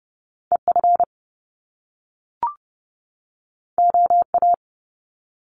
Morse code, heard as a keyed tone, about 30 wpm, 700 Hz. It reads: EF OA
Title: Top Two Letter Combinations in Words 30wpm